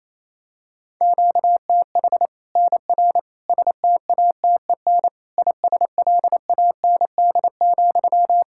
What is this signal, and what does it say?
Morse code, heard as a keyed tone, about 28 wpm, 700 Hz. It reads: QTH NR STATEN ISLAND,
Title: Practice QSO - Exchange 1 - Beginning - 28wpm